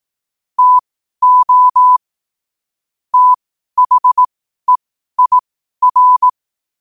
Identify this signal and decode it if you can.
Morse code, heard as a keyed tone, about 18 wpm, spaced as for 12 wpm, 1000 Hz. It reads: TO THEIR
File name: qotdfiles / 12wpm